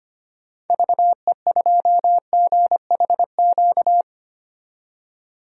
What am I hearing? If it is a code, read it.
Morse code, heard as a keyed tone, about 25 wpm, 700 Hz. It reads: VE2GHQ
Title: DXCC-100 Call Signs - 6 Characters - 25wpm